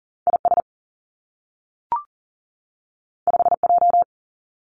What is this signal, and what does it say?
Morse code, heard as a keyed tone, about 41 wpm, 700 Hz. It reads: IS 5J
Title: Two Challenging Letter-Number Combinations 40wpm